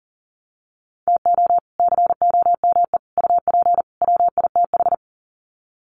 Morse code, heard as a keyed tone, about 37 wpm, 700 Hz. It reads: TO COME UP WITH
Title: Sets of 4 Words 40wpm